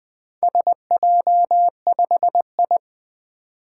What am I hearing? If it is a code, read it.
Morse code, heard as a keyed tone, about 21 wpm, 700 Hz. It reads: SJ5I
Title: Four Letter-Number Combinations - 20wpm